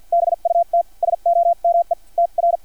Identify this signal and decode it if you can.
Morse code, heard as a keyed tone, about 50 wpm, spaced as for 42 wpm, 680 Hz. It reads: BUT SOME TR
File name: archived / CRSnet